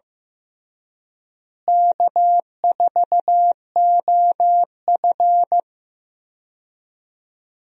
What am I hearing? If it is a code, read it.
Morse code, heard as a keyed tone, about 15 wpm, 700 Hz. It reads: K4OF
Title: USA Call Signs - 4 Characters - Repeated 2x - 15wpm